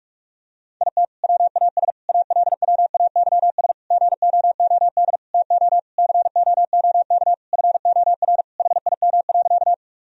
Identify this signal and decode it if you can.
Morse code, heard as a keyed tone, about 46 wpm, 700 Hz. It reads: IT WAS ALWAYS GOOD TO COOK FOR HIM.
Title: Sentences from Top 1000 Words - Review Part 4 - 45wpm